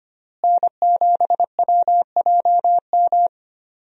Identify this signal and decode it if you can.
Morse code, heard as a keyed tone, about 25 wpm, 700 Hz. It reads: N7WJM
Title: USA Call Signs - 5 Characters - Repeated 2x - 25wpm